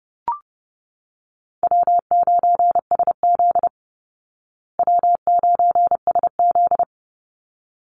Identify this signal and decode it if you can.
Morse code, heard as a keyed tone, about 30 wpm, 700 Hz. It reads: W9SZ W9SZ
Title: Call Signs Repeated 3x - 30wpm